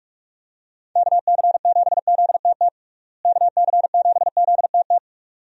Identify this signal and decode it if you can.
Morse code, heard as a keyed tone, about 41 wpm, 700 Hz. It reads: KC7ZTT KC7ZTT
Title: USA Call Signs - 6 Characters - Repeated 3x - 45wpm